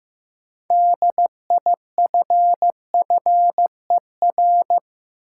Morse code, heard as a keyed tone, about 15 wpm, 700 Hz. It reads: DIFFER
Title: Sentences from Top 1000 Words - Review Part 1 - New Word Drill - 15wpm